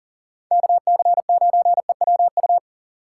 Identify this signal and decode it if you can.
Morse code, heard as a keyed tone, about 49 wpm, 700 Hz. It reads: KC9EWU
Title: Speed Racing - USA Call Signs 40wpm